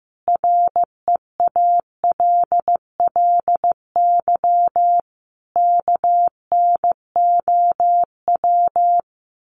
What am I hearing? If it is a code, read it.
Morse code, heard as a keyed tone, about 15 wpm, 700 Hz. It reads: REALLY KNOW
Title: Sets of 5 Words Encore 15wpm